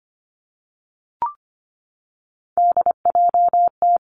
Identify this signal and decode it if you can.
Morse code, heard as a keyed tone, about 25 wpm, 700 Hz. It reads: DJT
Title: Three Challenging Letter-Number Combinations 25wpm